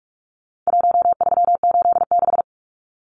Morse code, heard as a keyed tone, about 44 wpm, 700 Hz. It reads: 1386
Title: Numbers - Contest Style - 4-Digit 45wpm